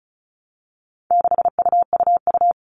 Morse code, heard as a keyed tone, about 34 wpm, 700 Hz. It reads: 6UUU
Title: All Cut Numbers - 4-Digit 35wpm